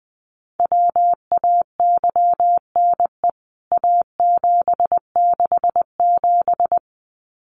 Morse code, heard as a keyed tone, about 20 wpm, 700 Hz. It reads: WAYNE A767
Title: CWT Exchange 20wpm